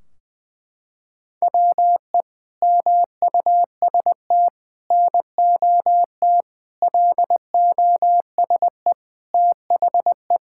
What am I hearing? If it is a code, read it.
Morse code, heard as a keyed tone, about 21 wpm, 700 Hz. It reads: WE MUST NOT LOSE THE
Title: Sentences from Top 1600 Words 20wpm